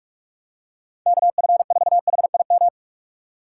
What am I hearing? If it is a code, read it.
Morse code, heard as a keyed tone, about 44 wpm, 700 Hz. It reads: KF4HIM